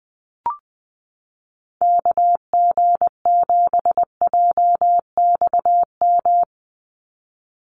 Morse code, heard as a keyed tone, about 20 wpm, 700 Hz. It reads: KG7JXM
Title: USA Call Signs 20wpm